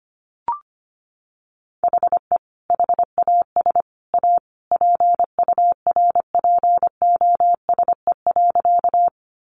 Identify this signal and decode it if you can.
Morse code, heard as a keyed tone, about 25 wpm, 700 Hz. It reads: HE HAS A PURPOSE.
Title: Sentences from Top 800 Words 25wpm